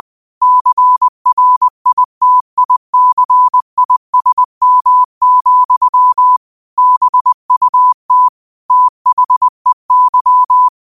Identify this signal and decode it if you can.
Morse code, heard as a keyed tone, about 20 wpm, 1000 Hz. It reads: CRITICISM, BUT THEY
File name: qotdfiles / 20wpm